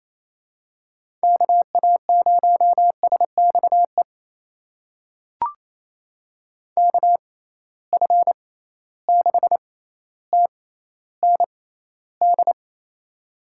Morse code, heard as a keyed tone, about 28 wpm, spaced as for 14 wpm, 700 Hz. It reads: KA0SXE KF6TND
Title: USA Call Signs - 6 Characters - 6x Character Spacing - 28wpm